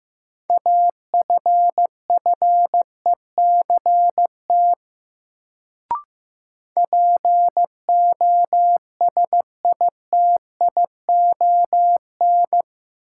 Morse code, heard as a keyed tone, about 15 wpm, 700 Hz. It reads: AFFECT POSITION
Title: Sentences from Top 900 Words - New Word Drill - 15wpm